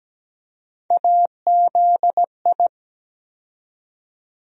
Morse code, heard as a keyed tone, about 17 wpm, 700 Hz. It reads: AZI
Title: USA Call Sign Postfixes - 3 Letters 17wpm